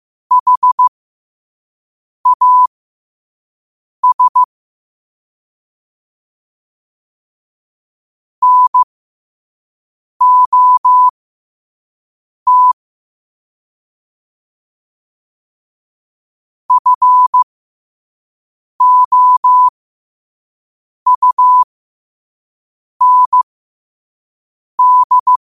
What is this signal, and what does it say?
Morse code, heard as a keyed tone, about 15 wpm, spaced as for 5 wpm, 1000 Hz. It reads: HAS NOT FOUND